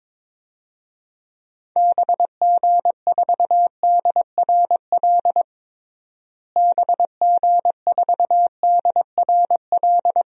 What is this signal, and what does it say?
Morse code, heard as a keyed tone, about 22 wpm, 700 Hz. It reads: BG4DRL BG4DRL
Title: DXCC-100 Call Signs - 6 Characters - Repeated 3x - 22wpm